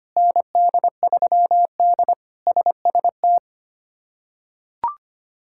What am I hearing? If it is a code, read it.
Morse code, heard as a keyed tone, about 25 wpm, 700 Hz. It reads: ND3D SST
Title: K1USN SST QSO Practice - 25wpm